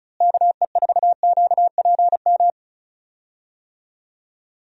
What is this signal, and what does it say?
Morse code, heard as a keyed tone, about 36 wpm, 700 Hz. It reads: KE4QPM
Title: USA Call Signs - 6 Characters - Repeated 3x - 35wpm